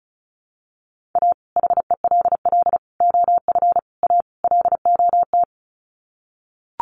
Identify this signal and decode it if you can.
Morse code, heard as a keyed tone, about 34 wpm, 700 Hz. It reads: A HELL OF A LOT
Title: Sets of 5 Words Encore 35wpm